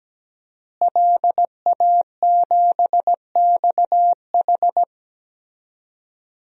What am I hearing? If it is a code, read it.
Morse code, heard as a keyed tone, about 17 wpm, 700 Hz. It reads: LA7XH
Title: DXCC-100 Call Signs - 5 Characters - Repeated 3x - 17wpm